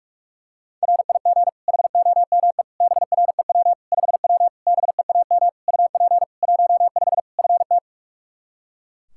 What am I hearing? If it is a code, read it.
Morse code, heard as a keyed tone, about 41 wpm, 700 Hz. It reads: RIG HOME BREW 5W BEAM UP 15 FT
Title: K1USN SST QSO Practice 45wpm